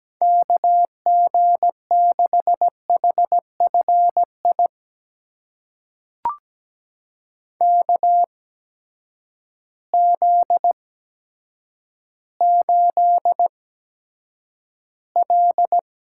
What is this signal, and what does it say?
Morse code, heard as a keyed tone, about 17 wpm, spaced as for 9 wpm, 700 Hz. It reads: KG6HFI KZ8L
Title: USA Call Signs - 8x Character Spacing - 17wpm